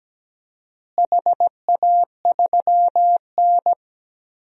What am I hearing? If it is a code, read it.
Morse code, heard as a keyed tone, about 17 wpm, 700 Hz. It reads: HA3N
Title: DXCC-100 Call Signs - 4 Characters - Repeated 3x - 17wpm